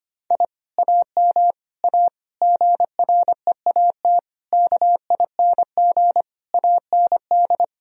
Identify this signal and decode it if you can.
Morse code, heard as a keyed tone, about 25 wpm, 700 Hz. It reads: I AM A GREAT KING AND
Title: Sentences from Top 1000 Words - Review Part 4 - 25wpm